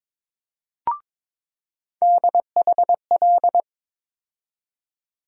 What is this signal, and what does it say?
Morse code, heard as a keyed tone, about 22 wpm, 700 Hz. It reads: DHL
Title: USA Call Sign Postfixes - 3 Letters 22wpm